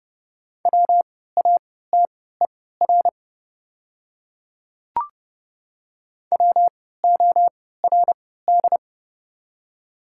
Morse code, heard as a keyed tone, about 32 wpm, spaced as for 17 wpm, 700 Hz. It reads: WATER WORD
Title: 3x Character Spacing - Top 100 Words - 30wpm